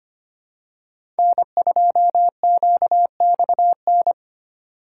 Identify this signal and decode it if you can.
Morse code, heard as a keyed tone, about 25 wpm, 700 Hz. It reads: N2QXN
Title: USA Call Signs - 5 Characters - 3x Character Spacing - 25wpm